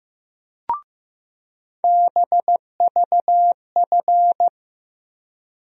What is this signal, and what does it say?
Morse code, heard as a keyed tone, about 15 wpm, 700 Hz. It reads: BVF
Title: USA Call Sign Postfixes - 3 Letters 15wpm